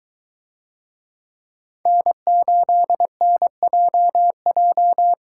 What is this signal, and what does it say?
Morse code, heard as a keyed tone, about 22 wpm, 700 Hz. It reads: N8NJJ